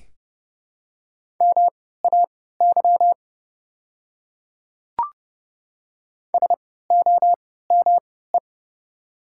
Morse code, heard as a keyed tone, about 30 wpm, spaced as for 17 wpm, 700 Hz. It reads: MAY SOME
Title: 3x Character Spacing - Top 100 Words - 30wpm